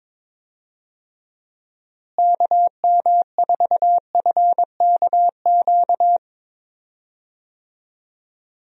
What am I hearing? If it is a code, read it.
Morse code, heard as a keyed tone, about 22 wpm, 700 Hz. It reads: KM4FKQ
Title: USA Call Signs - 6 Characters - Repeated 2x - 22wpm